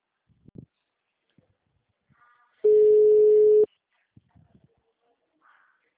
Ringback tone (Europe, 425 Hz).